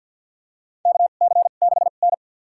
Morse code, heard as a keyed tone, about 52 wpm, spaced as for 37 wpm, 700 Hz. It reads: KC6N
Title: USA Call Signs - 4 Characters - 2x Character Spacing - 50wpm